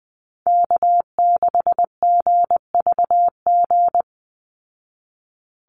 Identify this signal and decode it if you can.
Morse code, heard as a keyed tone, about 20 wpm, 700 Hz. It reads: K6GVG
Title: Call Signs Repeated 3x - 20wpm